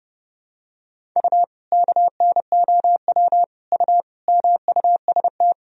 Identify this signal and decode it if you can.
Morse code, heard as a keyed tone, about 30 wpm, 700 Hz. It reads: U KNOW U MUST
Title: Sentences from Top 500 Words and Abbreviations 30wpm